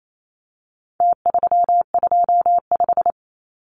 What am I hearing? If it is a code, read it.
Morse code, heard as a keyed tone, about 28 wpm, 700 Hz. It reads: T325
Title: Numbers - Contest Style - 4-Digit 28wpm